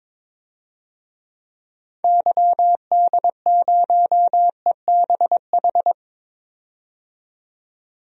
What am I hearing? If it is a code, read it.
Morse code, heard as a keyed tone, about 22 wpm, 700 Hz. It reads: YD0EBH
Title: DXCC-100 Call Signs - 6 Characters - 22wpm